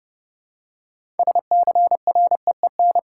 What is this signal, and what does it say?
Morse code, heard as a keyed tone, about 30 wpm, 700 Hz. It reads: SCREEN
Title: Six Letter Words - 30wpm